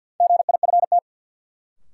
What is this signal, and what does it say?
Morse code, heard as a keyed tone, about 53 wpm, 700 Hz. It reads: GIFT
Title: Sentences from Top 900 Words - New Word Drill - 50wpm